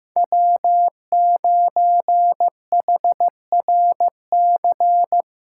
Morse code, heard as a keyed tone, about 15 wpm, 700 Hz. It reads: W9HRC
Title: USA Callsign DE Callsign - 5 Characters - 15wpm